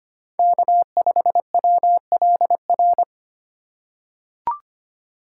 Morse code, heard as a keyed tone, about 25 wpm, 700 Hz. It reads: K5WLR
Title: USA Call Signs - 5 Characters - 4x Character Spacing - 25wpm